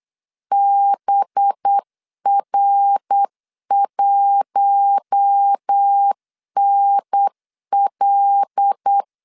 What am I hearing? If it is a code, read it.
Morse code, heard as a keyed tone, about 8 wpm, 800 Hz. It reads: BR1NL
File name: CW / DET / 1 9 1500C S A 8wpm 5637